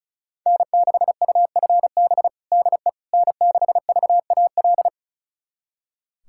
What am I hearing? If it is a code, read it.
Morse code, heard as a keyed tone, about 36 wpm, 700 Hz. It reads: N6UFB DE N6VAL
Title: USA Callsign DE Callsign - 5 Characters - 35wpm